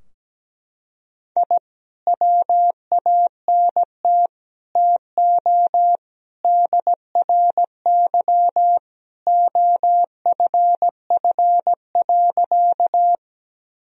Morse code, heard as a keyed tone, about 17 wpm, 700 Hz. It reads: I WANT TO DRY OFF.
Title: Sentences from Top 1000 Words - Review Part 3 - 17wpm